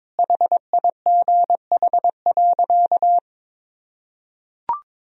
Morse code, heard as a keyed tone, about 22 wpm, 700 Hz. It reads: HIGH.
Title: Sentences from Top 200 Words and Abbreviations 22wpm